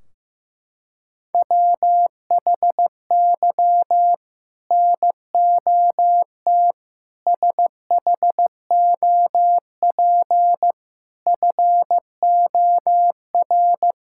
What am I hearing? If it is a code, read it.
Morse code, heard as a keyed tone, about 15 wpm, 700 Hz. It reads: WHY NOT SHOP FOR
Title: Sentences from Top 1600 Words 15wpm